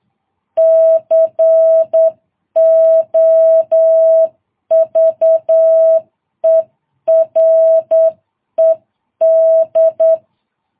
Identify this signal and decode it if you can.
Morse code, heard as a keyed tone, about 9 wpm, 650 Hz. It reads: COVERED
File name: jh8cbh